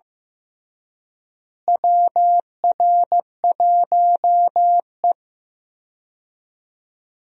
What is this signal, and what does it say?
Morse code, heard as a keyed tone, about 15 wpm, 700 Hz. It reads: WR1E
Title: USA Call Signs - 4 Characters - Repeated 3x - 15wpm